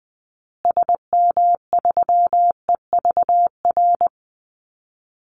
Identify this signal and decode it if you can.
Morse code, heard as a keyed tone, about 20 wpm, 700 Hz. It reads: SM3EVR
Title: Call Signs Repeated 2x - 20wpm